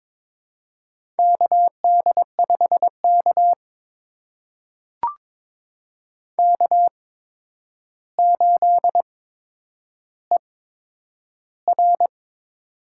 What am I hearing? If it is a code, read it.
Morse code, heard as a keyed tone, about 22 wpm, spaced as for 9 wpm, 700 Hz. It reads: KD5K K8ER
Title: USA Call Signs - 4 Characters - 8x Character Spacing - 22wpm